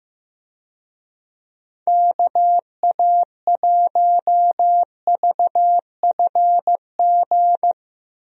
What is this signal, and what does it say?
Morse code, heard as a keyed tone, about 15 wpm, 700 Hz. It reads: KA1VFG
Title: USA Call Signs - 6 Characters - 15wpm